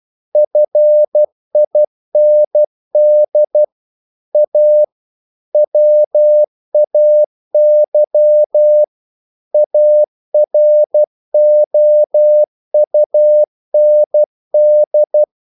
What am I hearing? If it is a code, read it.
Morse code, heard as a keyed tone, about 12 wpm, 600 Hz. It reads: FIND A WAY AROUND
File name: CW / Practice / EnglishQuotations0112